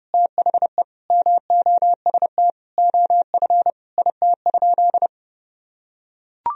Morse code, heard as a keyed tone, about 30 wpm, 700 Hz. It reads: THE MOST OF IT?
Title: Sentences from Top 1000 Words - Review Part 1 - 30wpm